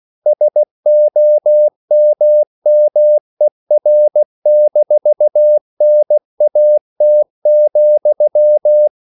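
Morse code, heard as a keyed tone, about 16 wpm, 600 Hz. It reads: SOMMER-NAT,
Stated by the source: Slow (QRS) Morse code at 16 wpm, from CW med Gnister.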